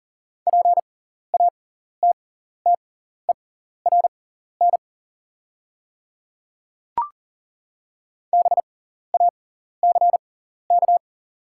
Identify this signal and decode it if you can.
Morse code, heard as a keyed tone, about 42 wpm, spaced as for 14 wpm, 690 Hz. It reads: PATTERN BACK
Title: 6x Character Spacing - Top 500 Words - 40wpm